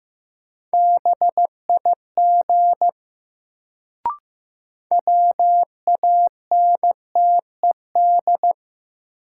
Morse code, heard as a keyed tone, about 15 wpm, 700 Hz. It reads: BIG WANTED
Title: Sets of 2 Words - New Word Drill - 15wpm